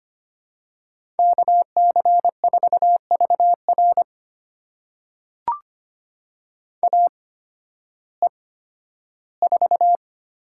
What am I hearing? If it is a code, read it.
Morse code, heard as a keyed tone, about 25 wpm, spaced as for 13 wpm, 700 Hz. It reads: KC4VR AE4
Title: USA Call Signs - 5 Characters - 8x Character Spacing - 25wpm